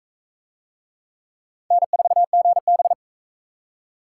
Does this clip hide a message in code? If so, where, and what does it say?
Yes: Morse code from 1.7 to 2.9 s, heard as a keyed tone, about 39 wpm, 700 Hz. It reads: N4GB